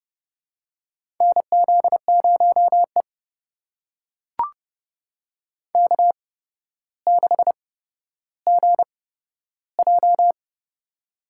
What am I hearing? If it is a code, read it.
Morse code, heard as a keyed tone, about 29 wpm, spaced as for 13 wpm, 700 Hz. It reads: NZ0E K6GJ
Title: USA Call Signs - 4 Characters - 8x Character Spacing - 30wpm